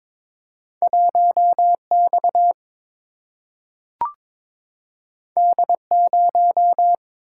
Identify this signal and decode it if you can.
Morse code, heard as a keyed tone, about 22 wpm, 700 Hz. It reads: 1 X D 0
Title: Single-Letter Single-Number 22wpm